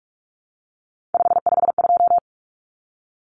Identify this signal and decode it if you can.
Morse code, heard as a keyed tone, about 44 wpm, 700 Hz. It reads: 552
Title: Numbers - 3-Digit 45wpm